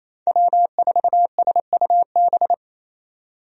Morse code, heard as a keyed tone, about 28 wpm, 700 Hz. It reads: W4SUB
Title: USA Call Signs - 5 Characters - 28wpm